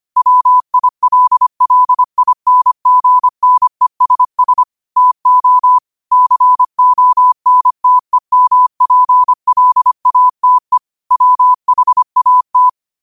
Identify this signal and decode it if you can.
Morse code, heard as a keyed tone, about 25 wpm, 1000 Hz. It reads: WILLINGNESS TO CONTEMPLATE WHAT
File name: qotdfiles / 25wpm